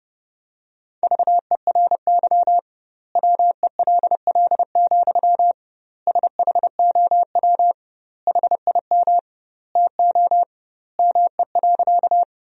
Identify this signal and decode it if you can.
Morse code, heard as a keyed tone, about 30 wpm, 700 Hz. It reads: VERY WELL, SHOW HIM TO ME.
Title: 2x Word-Spacing - Sentences from Top 200 Words - 30wpm